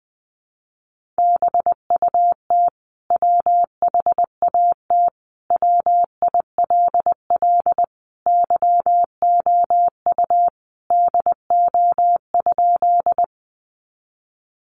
Morse code, heard as a keyed tone, about 20 wpm, 700 Hz. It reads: BUT WHAT WILL YOU DO?
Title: Sentences from Top 100 Words 20wpm